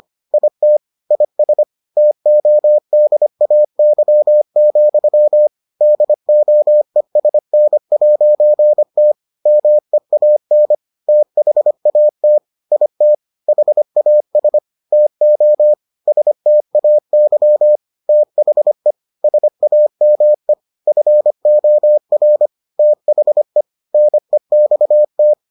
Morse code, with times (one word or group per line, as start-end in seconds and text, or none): 0.3-0.8 IT
1.1-1.6 IS
2.0-5.5 TODAY,
5.8-9.1 DOESN'T
9.5-10.7 MEAN
11.1-12.4 THAT
12.7-13.1 IT
13.5-14.6 HAS
14.9-15.7 TO
16.1-17.8 STAY
18.1-18.9 THE
19.2-20.5 SAME
20.9-22.5 FOR
22.8-23.6 THE
23.9-25.3 NEXT